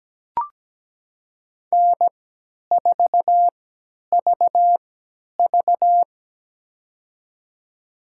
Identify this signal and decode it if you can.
Morse code, heard as a keyed tone, about 17 wpm, spaced as for 10 wpm, 700 Hz. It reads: N4VV
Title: USA Call Signs - 4 Characters - 3x Character Spacing - 17wpm